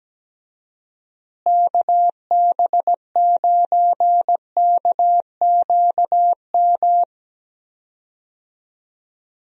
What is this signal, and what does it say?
Morse code, heard as a keyed tone, about 17 wpm, 700 Hz. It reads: KB9KQM
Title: USA Call Signs - 6 Characters - Repeated 2x - 17wpm